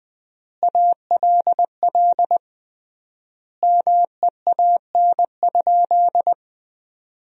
Morse code, heard as a keyed tone, about 20 wpm, 700 Hz. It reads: ALL MEAN?
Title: 3x Word-Spacing - Sentences from Top 200 Words - 20wpm